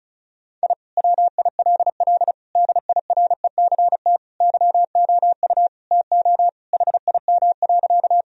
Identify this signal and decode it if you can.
Morse code, heard as a keyed tone, about 34 wpm, 700 Hz. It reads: I WILL DIRECT YOU TO HIM.
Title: Sentences from Top 1000 Words - Review Part 2 - 35wpm